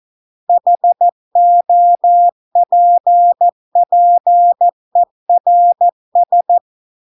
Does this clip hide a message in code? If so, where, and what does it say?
Yes: Morse code from 0.5 to 6.6 s, heard as a keyed tone, about 14 wpm, 700 Hz. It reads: HOPPERS